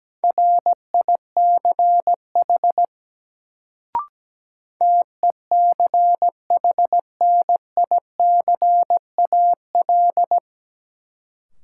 Morse code, heard as a keyed tone, about 17 wpm, 700 Hz. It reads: RICH TECHNICAL
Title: Sentences from Top 1100 Words - New Word Drill Alt - 17wpm